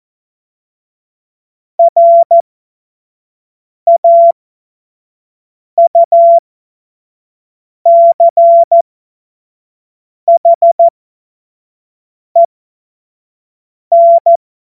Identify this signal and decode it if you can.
Morse code, heard as a keyed tone, about 14 wpm, spaced as for 5 wpm, 680 Hz. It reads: RAUCHEN
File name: Morsekurs / Witze / Witz 25. Januar 2022, 25 F70